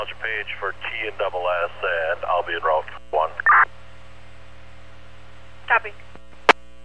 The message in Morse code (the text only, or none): none